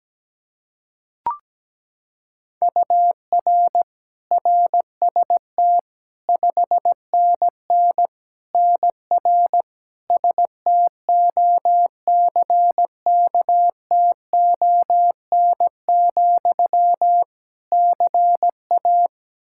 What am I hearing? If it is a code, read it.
Morse code, heard as a keyed tone, about 17 wpm, 700 Hz. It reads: UR RST 5NN NR STOCKTON, CA
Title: Practice QSO - Exchange 1 - Beginning - 17wpm